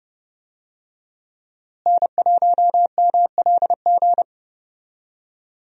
Morse code, heard as a keyed tone, about 31 wpm, 700 Hz. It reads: N1MLG
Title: USA Call Signs - 5 Characters - Repeated 2x - 30wpm